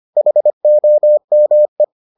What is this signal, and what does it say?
Morse code, heard as a keyed tone, about 25 wpm, 600 Hz. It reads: HOME